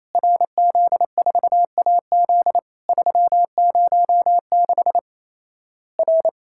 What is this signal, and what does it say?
Morse code, heard as a keyed tone, about 28 wpm, 700 Hz. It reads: RZ4AZ 306 R
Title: Worked All Europe DX Contest - QTC Exchange - 28wpm